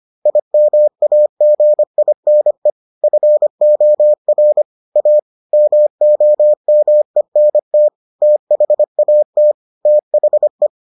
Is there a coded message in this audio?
Yes: Morse code from 0.3 to 10.7 s, heard as a keyed tone, about 25 wpm, 600 Hz. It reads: IMAGINE FOR A MOMENT THAT THE